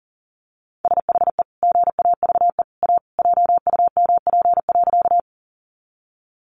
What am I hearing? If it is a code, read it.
Morse code, heard as a keyed tone, about 40 wpm, 700 Hz. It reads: SHE GAVE A JUMP.
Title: Sentences from Top 1300 Words 40wpm